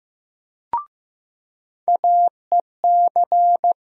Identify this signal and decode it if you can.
Morse code, heard as a keyed tone, about 15 wpm, 700 Hz. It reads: AEC
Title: USA Call Sign Postfixes - 3 Letters 15wpm